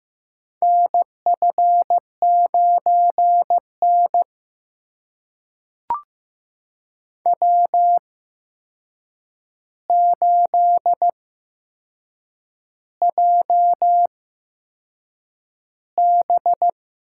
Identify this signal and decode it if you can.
Morse code, heard as a keyed tone, about 15 wpm, spaced as for 6 wpm, 700 Hz. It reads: NF9N W8JB
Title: USA Call Signs - 4 Characters - 8x Character Spacing - 15wpm